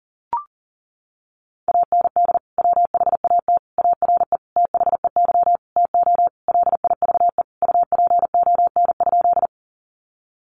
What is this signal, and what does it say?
Morse code, heard as a keyed tone, about 40 wpm, 700 Hz. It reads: AND WHAT ARE THEY TO LIVE UPON?
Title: Sentences from Top 1100 Words 40wpm